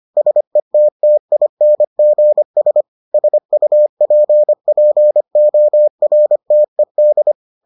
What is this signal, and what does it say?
Morse code, heard as a keyed tone, about 25 wpm, 600 Hz. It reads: SETTINGS SUPPORTED